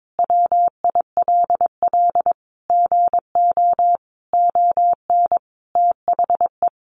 Morse code, heard as a keyed tone, about 22 wpm, 700 Hz. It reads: WILL GO ON THE
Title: Sentences from Top 1400 Words 22wpm